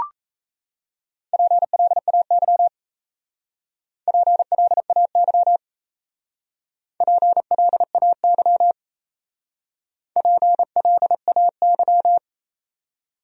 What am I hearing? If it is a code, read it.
Morse code, heard as a keyed tone, about 36 wpm, 700 Hz. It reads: PLAY PLAY PLAY PLAY